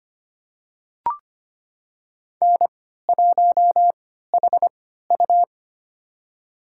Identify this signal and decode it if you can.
Morse code, heard as a keyed tone, about 25 wpm, spaced as for 14 wpm, 700 Hz. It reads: N1HU
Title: USA Call Signs - 4 Characters - 3x Character Spacing - 25wpm